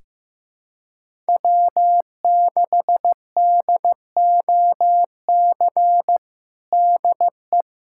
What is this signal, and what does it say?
Morse code, heard as a keyed tone, about 15 wpm, 700 Hz. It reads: W6DOC DE
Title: USA Callsign DE Callsign - 5 Characters - 15wpm